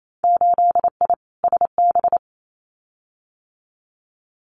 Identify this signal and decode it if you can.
Morse code, heard as a keyed tone, about 28 wpm, 700 Hz. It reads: 8I SB
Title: Winter Field Day Exchanges Encore 28wpm